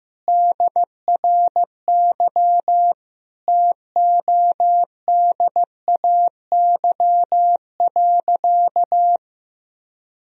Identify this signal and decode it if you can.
Morse code, heard as a keyed tone, about 15 wpm, 700 Hz. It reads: DRY TODAY.
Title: Sentences from Top 1000 Words - Review Part 3 - 15wpm